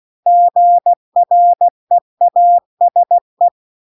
Morse code, heard as a keyed tone, about 16 wpm, 700 Hz. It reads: GREASE